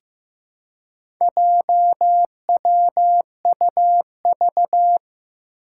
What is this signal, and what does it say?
Morse code, heard as a keyed tone, about 15 wpm, 700 Hz. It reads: JWUV
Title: Four Challenging Letter-Number Combinations 15wpm